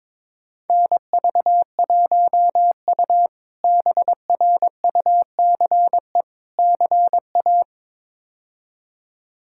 Morse code, heard as a keyed tone, about 22 wpm, 700 Hz. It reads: NV1U BRUCE CA
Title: K1USN SST QSO Practice - 22wpm